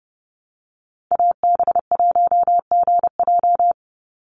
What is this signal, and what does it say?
Morse code, heard as a keyed tone, about 30 wpm, 700 Hz. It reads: AB1GJ
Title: Call Signs Repeated 3x - 30wpm